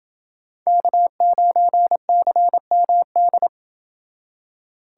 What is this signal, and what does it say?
Morse code, heard as a keyed tone, about 28 wpm, 700 Hz. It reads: K9CMD